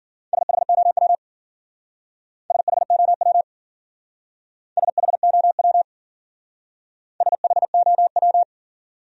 Morse code, heard as a keyed tone, about 54 wpm, 700 Hz. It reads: SHOW SHOW SHOW SHOW